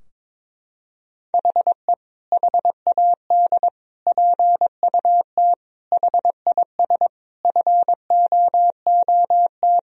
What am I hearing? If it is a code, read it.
Morse code, heard as a keyed tone, about 22 wpm, 700 Hz. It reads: HE HAD PUT HIS FOOT
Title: Sentences from Top 1000 Words - Review Part 3 - 22wpm